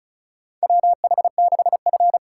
Morse code, heard as a keyed tone, about 36 wpm, 700 Hz. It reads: WH6F
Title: USA Call Signs - 4x Character Spacing - 35wpm